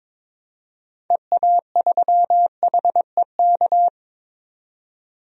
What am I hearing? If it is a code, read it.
Morse code, heard as a keyed tone, about 22 wpm, 700 Hz. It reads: EA3HEK
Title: DXCC-100 Call Signs - 6 Characters - 22wpm